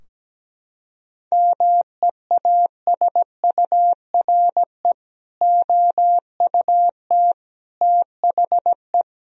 Morse code, heard as a keyed tone, about 17 wpm, 700 Hz. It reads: MEASURE OUT THE
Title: Sentences from Top 1000 Words - Review Part 2 - 17wpm